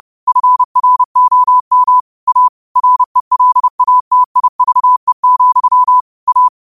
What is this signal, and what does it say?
Morse code, heard as a keyed tone, about 30 wpm, 1000 Hz. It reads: FROM A RELATIVE, A
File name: qotdfiles / 30wpm